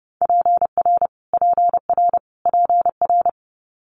Morse code, heard as a keyed tone, about 30 wpm, 690 Hz. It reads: PR PR PR
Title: Multi-letter Phonemes Repeated 3x 30wpm